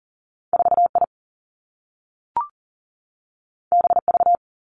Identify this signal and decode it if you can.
Morse code, heard as a keyed tone, about 39 wpm, 700 Hz. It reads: 4 I B V
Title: Two Challenging Letter-Number Combinations 40wpm